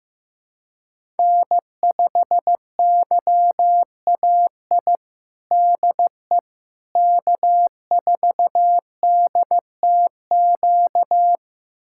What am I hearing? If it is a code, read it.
Morse code, heard as a keyed tone, about 15 wpm, 700 Hz. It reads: N5YAI DE K4DTQ
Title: USA Callsign DE Callsign 15wpm